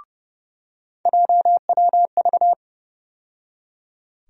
Morse code, heard as a keyed tone, about 30 wpm, 700 Hz. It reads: JWV
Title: USA Call Sign Postfixes - 3 Letters 30wpm